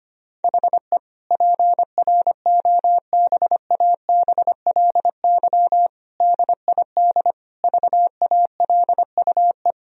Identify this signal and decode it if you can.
Morse code, heard as a keyed tone, about 25 wpm, 700 Hz. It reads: HE PROBABLY DID VALUE
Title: Sentences from Top 1000 Words - Review Part 4 - 25wpm